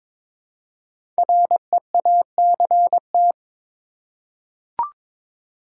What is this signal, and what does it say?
Morse code, heard as a keyed tone, about 22 wpm, 700 Hz. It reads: REACT
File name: 5 Letter Words - First 3 Letters Given - 22wpm